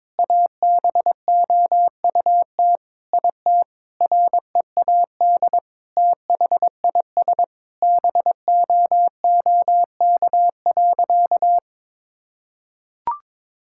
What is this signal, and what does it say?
Morse code, heard as a keyed tone, about 22 wpm, 700 Hz. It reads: ABOUT IT READ THIS BOOK.
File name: Sentences with 7-Word Phrases - 22wpm